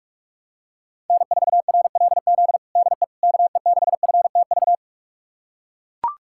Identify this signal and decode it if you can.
Morse code, heard as a keyed tone, about 41 wpm, 700 Hz. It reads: N4RLZ DE KE6FTV
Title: USA Callsign DE Callsign 45wpm